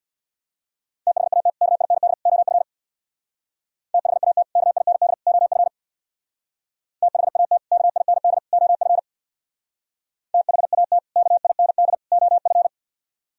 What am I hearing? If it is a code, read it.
Morse code, heard as a keyed tone, about 63 wpm, 700 Hz. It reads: THAT KIND OF THAT KIND OF THAT KIND OF THAT KIND OF